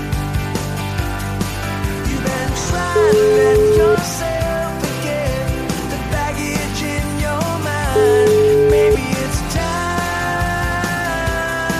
Ringback tone (Europe, 425 Hz), with noise.